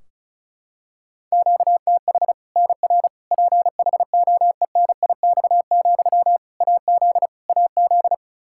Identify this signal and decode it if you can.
Morse code, heard as a keyed tone, about 36 wpm, 700 Hz. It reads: QTH NR PHOENIX, AZ AZ
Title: Practice QSO - Exchange 1 - Beginning - 35wpm